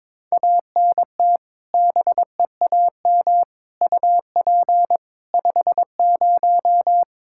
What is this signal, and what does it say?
Morse code, heard as a keyed tone, about 22 wpm, 700 Hz. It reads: ANT BEAM UP 50
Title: Practice QSO - Exchange 2 - Info - 22wpm